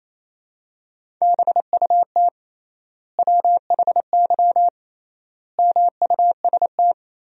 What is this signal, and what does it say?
Morse code, heard as a keyed tone, about 28 wpm, 700 Hz. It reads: BUT WHY MUST
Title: 3x Word-Spacing - Sentences from Top 400 Words - 28wpm